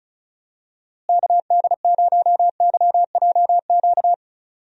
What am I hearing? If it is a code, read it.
Morse code, heard as a keyed tone, about 36 wpm, 700 Hz. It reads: KD0YJQ
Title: USA Call Signs - 6 Characters - Repeated 3x - 35wpm